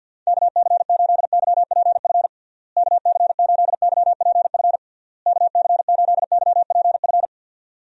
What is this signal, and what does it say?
Morse code, heard as a keyed tone, about 51 wpm, 700 Hz. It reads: KC8YPF KC8YPF KC8YPF
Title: USA Call Signs - 6 Characters - Repeated 3x - 50wpm